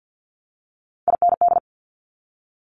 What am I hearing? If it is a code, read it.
Morse code, heard as a keyed tone, about 50 wpm, 690 Hz. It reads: IND
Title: Multi-letter Phonemes 50wpm